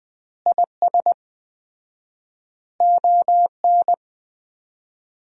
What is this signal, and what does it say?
Morse code, heard as a keyed tone, about 20 wpm, 700 Hz. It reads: IS ON
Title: 4x Word-Spacing - Sentences from Top 400 Words - 20wpm